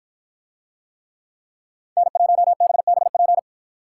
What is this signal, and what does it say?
Morse code, heard as a keyed tone, about 53 wpm, 700 Hz. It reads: N1BBP